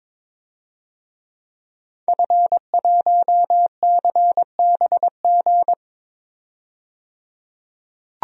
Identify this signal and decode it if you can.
Morse code, heard as a keyed tone, about 22 wpm, 700 Hz. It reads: F1CBG
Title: DXCC-100 Call Signs - 5 Characters - 22wpm